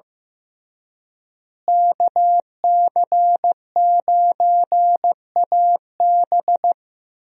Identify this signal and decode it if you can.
Morse code, heard as a keyed tone, about 16 wpm, 700 Hz. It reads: KC9AB
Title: USA Call Signs - 5 Characters - Repeated 3x - 15wpm